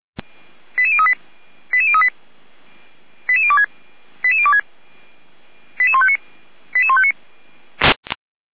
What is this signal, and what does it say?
Five-tone selective call (ZVEI), heard as quick runs of five tones, tones of about 70 ms. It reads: 88038 88038 88026 88026 88159 88159